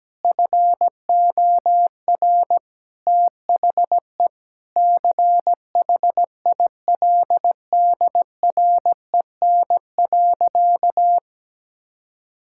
Morse code, heard as a keyed tone, about 17 wpm, 700 Hz. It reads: FOR THE CHILDREN.
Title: Sentences from Top 1000 Words - Review Part 6 - 17wpm